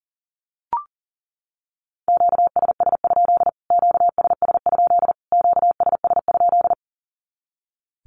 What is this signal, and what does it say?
Morse code, heard as a keyed tone, about 40 wpm, 700 Hz. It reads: QSS? QSS? QSS?
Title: Q-Codes Repeated 3x 40wpm